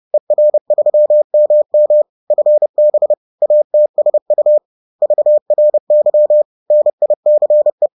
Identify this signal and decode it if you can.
Morse code, heard as a keyed tone, about 30 wpm, 600 Hz. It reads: ER3MM FB ATSU VRY NICE